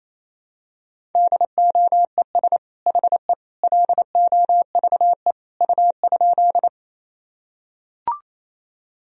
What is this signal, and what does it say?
Morse code, heard as a keyed tone, about 28 wpm, 700 Hz. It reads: DOES HE LOVE U?
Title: Sentences from Top 400 Words and Abbreviations 28wpm